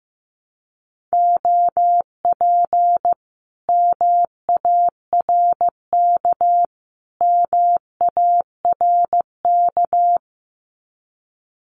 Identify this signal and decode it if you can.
Morse code, heard as a keyed tone, about 15 wpm, 700 Hz. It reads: OP MARK MARK
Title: Common QSO Phrases Encore 15wpm